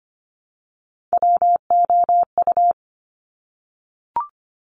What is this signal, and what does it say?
Morse code, heard as a keyed tone, about 25 wpm, 700 Hz. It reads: WOU
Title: Top Three Letter Combinations in Words 25wpm